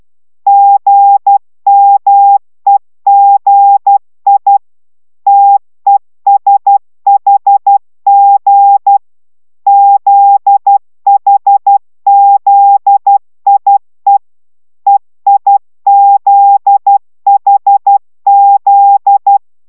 Morse code, heard as a keyed tone, about 12 wpm, 800 Hz. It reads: GMEGI TESHG ZHZIE EIZHZ